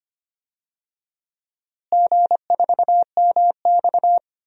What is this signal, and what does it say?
Morse code, heard as a keyed tone, about 25 wpm, 700 Hz. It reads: G4MX
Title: DXCC-100 Call Signs - 4 Characters - Repeated 3x - 25wpm